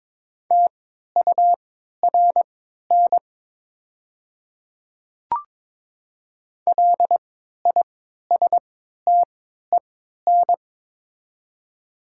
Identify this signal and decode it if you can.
Morse code, heard as a keyed tone, about 22 wpm, spaced as for 12 wpm, 700 Hz. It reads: TURN LISTEN
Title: 3x Character Spacing - Top 500 Words - 22wpm